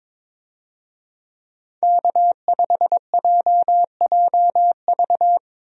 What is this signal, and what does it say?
Morse code, heard as a keyed tone, about 22 wpm, 700 Hz. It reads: K5JJV
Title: USA Call Signs - 5 Characters - Repeated 2x - 22wpm